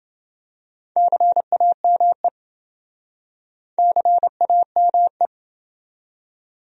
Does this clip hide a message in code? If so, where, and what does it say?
Yes: Morse code from 1.0 to 5.3 s, heard as a keyed tone, about 29 wpm, 700 Hz. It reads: CAME CAME